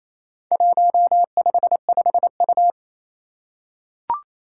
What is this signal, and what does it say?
Morse code, heard as a keyed tone, about 28 wpm, 700 Hz. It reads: 155U
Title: Four Challenging Letter-Number Combinations 28wpm